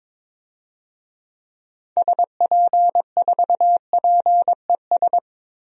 Morse code, heard as a keyed tone, about 22 wpm, 700 Hz. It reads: SP4PES
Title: DXCC-100 Call Signs - 6 Characters - Repeated 2x - 22wpm